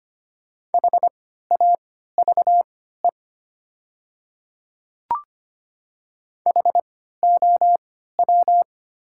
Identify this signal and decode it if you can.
Morse code, heard as a keyed tone, about 25 wpm, spaced as for 14 wpm, 700 Hz. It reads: HAVE HOW
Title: 3x Character Spacing - Top 100 Words - 25wpm